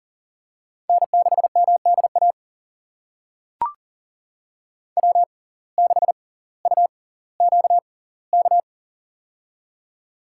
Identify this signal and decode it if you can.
Morse code, heard as a keyed tone, about 37 wpm, spaced as for 20 wpm, 700 Hz. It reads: N6MDA W6UQK
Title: USA Call Signs - 5 Characters - 6x Character Spacing - 40wpm